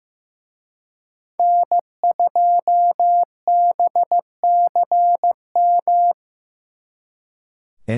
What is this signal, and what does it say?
Morse code, heard as a keyed tone, about 18 wpm, 700 Hz. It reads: N2BCM
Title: Speed Racing - USA 1x3 Call Signs 15wpm